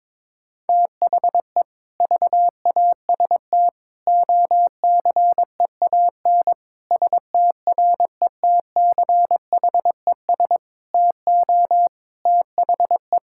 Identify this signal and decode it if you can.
Morse code, heard as a keyed tone, about 22 wpm, 700 Hz. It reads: THE VAST OCEAN STRETCHES TO THE
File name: Difficult Sentences with Hints - 22wpm